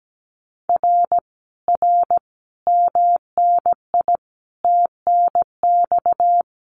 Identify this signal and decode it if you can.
Morse code, heard as a keyed tone, about 17 wpm, 700 Hz. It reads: R R MNI TNX
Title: Common QSO Phrases Encore 17wpm